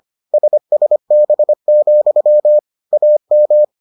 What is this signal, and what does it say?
Morse code, heard as a keyed tone, about 25 wpm, 600 Hz. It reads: SSB, AM